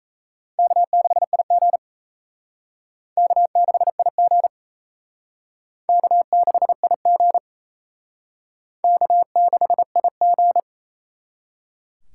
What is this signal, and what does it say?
Morse code, heard as a keyed tone, about 35 wpm, 700 Hz. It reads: K6IG K6IG K6IG K6IG
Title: Speed Racing - USA 1x2 Call Signs 28wpm